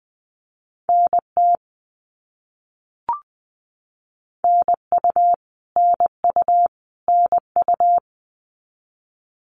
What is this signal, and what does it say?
Morse code, heard as a keyed tone, about 20 wpm, 700 Hz. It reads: NT NU NU NU
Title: Canadian Provinces and Territories - Repeated 3x - 20wpm